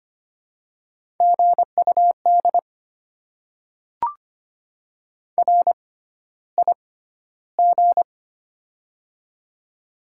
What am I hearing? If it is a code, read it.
Morse code, heard as a keyed tone, about 25 wpm, spaced as for 13 wpm, 700 Hz. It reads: GUD RIG
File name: Top 20 QSO Elements - 6x Character Spacing - 25wpm